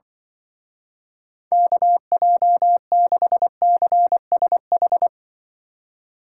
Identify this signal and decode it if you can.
Morse code, heard as a keyed tone, about 23 wpm, 700 Hz. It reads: KJ6CSH
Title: Speed Racing - USA 2x3 Call Signs 20wpm